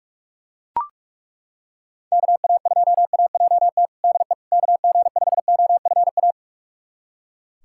Morse code, heard as a keyed tone, about 42 wpm, 700 Hz. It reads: KA2AJT DE KG5OFA
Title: USA Callsign DE Callsign - 6 Characters - 45wpm